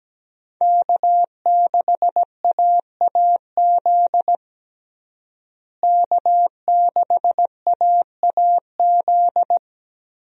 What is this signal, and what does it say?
Morse code, heard as a keyed tone, about 17 wpm, 700 Hz. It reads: K6AAZ K6AAZ
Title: USA Call Signs - 5 Characters - Repeated 3x - 17wpm